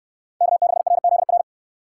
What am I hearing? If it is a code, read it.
Morse code, heard as a keyed tone, about 68 wpm, 700 Hz. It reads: K6RZR